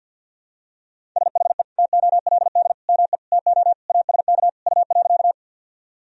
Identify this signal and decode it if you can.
Morse code, heard as a keyed tone, about 51 wpm, 700 Hz. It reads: SHE TOLD ME TO ASK U.
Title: Sentences from Top 400 Words and Abbreviations 50wpm